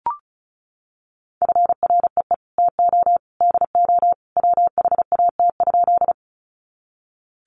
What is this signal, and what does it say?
Morse code, heard as a keyed tone, about 34 wpm, 700 Hz. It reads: FREE TO DO WHAT?
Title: Sentences from Top 500 Words 35wpm